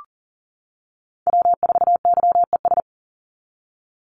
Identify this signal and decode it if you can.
Morse code, heard as a keyed tone, about 40 wpm, 700 Hz. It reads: W4YES
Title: Call Signs - 40wpm